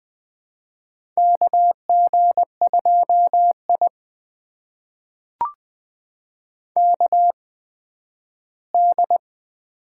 Morse code, heard as a keyed tone, about 20 wpm, spaced as for 12 wpm, 700 Hz. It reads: KG2I KD
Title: USA Call Signs - 8x Character Spacing - 20wpm